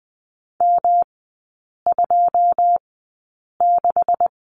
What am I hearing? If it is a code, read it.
Morse code, heard as a keyed tone, about 20 wpm, spaced as for 8 wpm, 700 Hz. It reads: M26
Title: 133 Lesson 22e - Question Mark - Three Characters 20wpm